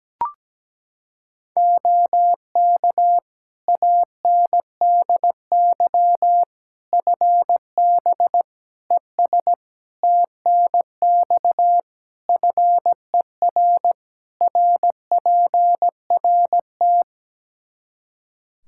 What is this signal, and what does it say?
Morse code, heard as a keyed tone, about 17 wpm, 700 Hz. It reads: OK ANDY FB ES TNX FER RPRT
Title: K1USN SST QSO Practice 17wpm